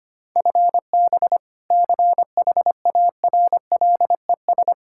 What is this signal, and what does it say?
Morse code, heard as a keyed tone, about 25 wpm, 700 Hz. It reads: FB CHARLES
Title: Practice QSO - Exchange 3 - More Info - 25wpm